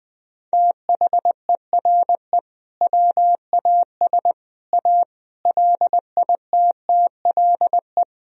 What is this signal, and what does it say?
Morse code, heard as a keyed tone, about 20 wpm, 700 Hz. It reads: THERE WAS A LITTLE
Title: Sentences from Top 1000 Words - Review Part 4 - 20wpm